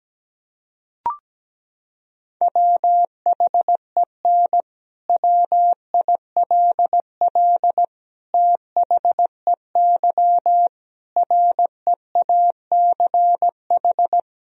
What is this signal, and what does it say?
Morse code, heard as a keyed tone, about 17 wpm, 700 Hz. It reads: WHEN WILL THEY REACH
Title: Sentences from Top 1000 Words - Review Part 4 - 17wpm